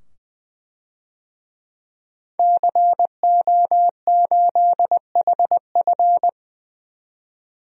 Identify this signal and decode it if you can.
Morse code, heard as a keyed tone, about 20 wpm, 700 Hz. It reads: CO8HF
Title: DXCC-100 Call Signs - 5 Characters - 20wpm